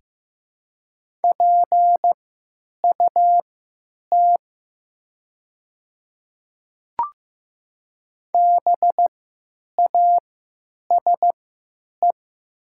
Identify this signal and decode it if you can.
Morse code, heard as a keyed tone, about 15 wpm, spaced as for 9 wpm, 700 Hz. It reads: PUT BASE
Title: 3x Character Spacing - Top 300 Words - 15wpm